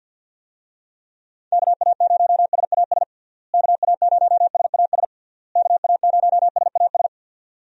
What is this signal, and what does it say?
Morse code, heard as a keyed tone, about 51 wpm, 700 Hz. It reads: KA0SAS KA0SAS KA0SAS
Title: USA Call Signs - 6 Characters - Repeated 3x - 50wpm